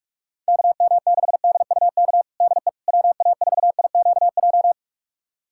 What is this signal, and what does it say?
Morse code, heard as a keyed tone, about 46 wpm, 700 Hz. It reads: KM6DUK DE WA4IQJ
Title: USA Callsign DE Callsign - 6 Characters - 45wpm